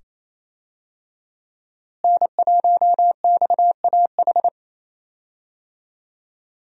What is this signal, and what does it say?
Morse code, heard as a keyed tone, about 28 wpm, 700 Hz. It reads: N1XAH
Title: USA Call Signs - 5 Characters - Repeated 2x - 28wpm